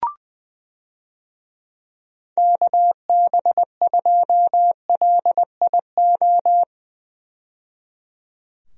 Morse code, heard as a keyed tone, about 20 wpm, 700 Hz. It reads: KB2LIO
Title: USA Call Signs - 6 Characters - 20wpm